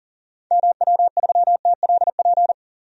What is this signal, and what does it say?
Morse code, heard as a keyed tone, about 38 wpm, 700 Hz. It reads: MW3TLP
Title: DXCC-100 Call Signs - 6 Characters - Repeated 2x - 40wpm